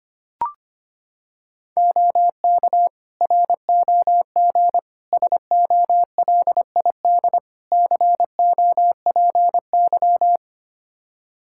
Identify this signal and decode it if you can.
Morse code, heard as a keyed tone, about 25 wpm, 700 Hz. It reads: OK ROG SOLID COPY
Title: Practice QSO - Exchange 3 - More Info - 25wpm